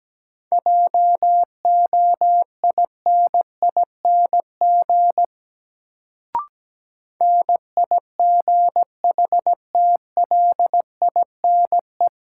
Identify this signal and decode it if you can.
Morse code, heard as a keyed tone, about 17 wpm, 700 Hz. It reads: JOINING NIGHTLINE
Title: Sets of 5 Words - New Word Drill - 17wpm